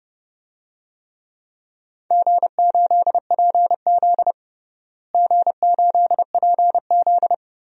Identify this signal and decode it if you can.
Morse code, heard as a keyed tone, about 30 wpm, 700 Hz. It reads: G8PZ G8PZ
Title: DXCC-100 Call Signs - 4 Characters - Repeated 2x - 30wpm